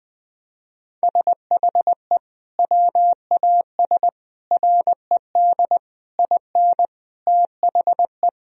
Morse code, heard as a keyed tone, about 20 wpm, 700 Hz. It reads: SHE WAS RED IN THE
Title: Sentences from Top 1000 Words - Review Part 2 - 20wpm